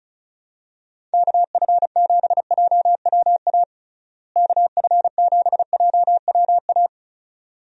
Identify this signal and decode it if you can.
Morse code, heard as a keyed tone, about 36 wpm, 700 Hz. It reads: KF7JWA KF7JWA
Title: USA Call Signs - 6 Characters - Repeated 2x - 35wpm